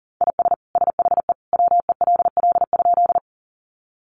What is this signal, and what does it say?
Morse code, heard as a keyed tone, about 40 wpm, 700 Hz. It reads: IS SHE WELL?
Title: Sentences from Top 200 Words 40wpm